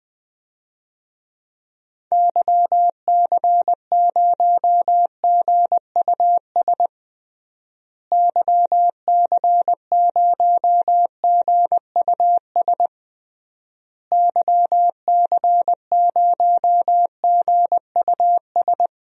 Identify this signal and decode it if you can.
Morse code, heard as a keyed tone, about 20 wpm, 700 Hz. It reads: YC0GUS YC0GUS YC0GUS
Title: DXCC-100 Call Signs - 6 Characters - Repeated 3x - 20wpm